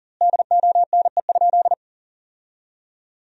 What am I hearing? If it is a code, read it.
Morse code, heard as a keyed tone, about 40 wpm, 700 Hz. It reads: DONE?
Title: 2x Word-Spacing - Sentences from Top 500 Words - 40wpm